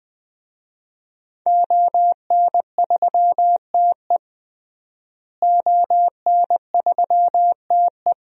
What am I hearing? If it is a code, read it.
Morse code, heard as a keyed tone, about 20 wpm, 700 Hz. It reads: ON3TE ON3TE
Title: DXCC-100 Call Signs - 5 Characters - Repeated 2x - 20wpm